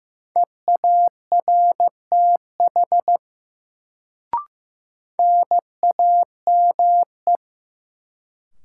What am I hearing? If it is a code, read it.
Morse code, heard as a keyed tone, about 16 wpm, 700 Hz. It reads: EARTH NAME
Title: Sentences from Top 200 Words - New Word Drill - 15wpm